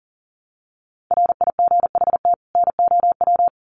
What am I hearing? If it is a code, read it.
Morse code, heard as a keyed tone, about 40 wpm, 700 Hz. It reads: RIGHT NOW
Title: Sets of 2 Words 40wpm